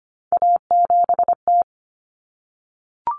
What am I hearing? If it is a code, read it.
Morse code, heard as a keyed tone, about 25 wpm, 700 Hz. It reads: A7T
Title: Common Cut Numbers - 3-Digit 25wpm